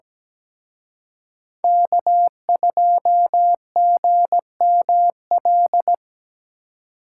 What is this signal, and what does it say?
Morse code, heard as a keyed tone, about 19 wpm, 700 Hz. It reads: K2GML
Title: Speed Racing - USA 1x3 Call Signs 17wpm